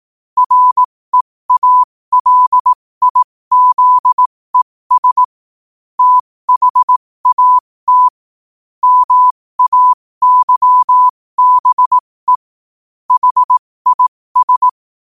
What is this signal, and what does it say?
Morse code, heard as a keyed tone, about 18 wpm, spaced as for 15 wpm, 1000 Hz. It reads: REALIZES THAT MAYBE HIS
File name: qotdfiles / 15wpm